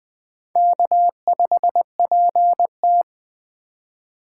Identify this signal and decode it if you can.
Morse code, heard as a keyed tone, about 20 wpm, 700 Hz. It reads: K5PT
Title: Speed Racing - USA 1x2 Call Signs 17wpm